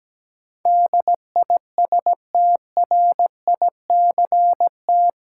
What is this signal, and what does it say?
Morse code, heard as a keyed tone, about 17 wpm, 700 Hz. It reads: DISTRICT
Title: Sentences from Top 1600 Words - New Word Drill Alt - 17wpm